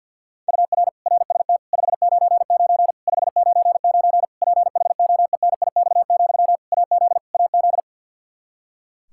Morse code, heard as a keyed tone, about 50 wpm, 700 Hz. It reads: UR RST 599 599 PHOENIX, AZ AZ
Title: K1USN SST QSO Practice 50wpm